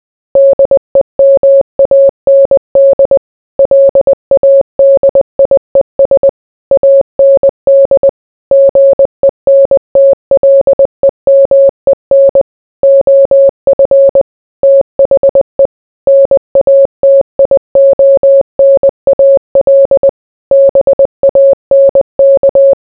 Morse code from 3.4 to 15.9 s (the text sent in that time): LADIES AND GENTLEMEN OF THE